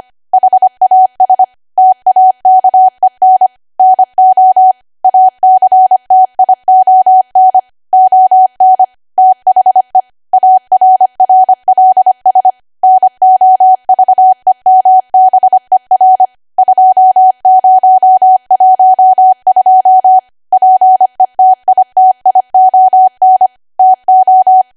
Morse code, heard as a keyed tone, about 25 wpm, 750 Hz. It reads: HAS TAKEN NO ACTION ON THE ARRLS NOVEMBER 2012 PETITION TO
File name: cwa / W1AW-Level3 / 25WPM